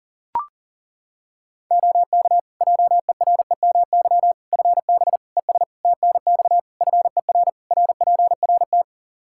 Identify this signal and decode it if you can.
Morse code, heard as a keyed tone, about 40 wpm, 700 Hz. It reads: OK JEREMY FB ES TNX FER RPRT
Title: Practice QSO - Exchange 2 - Info - 40wpm